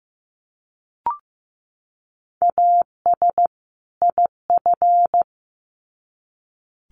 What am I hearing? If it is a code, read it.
Morse code, heard as a keyed tone, about 15 wpm, 700 Hz. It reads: AS IF
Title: Sets of 2 Words 15wpm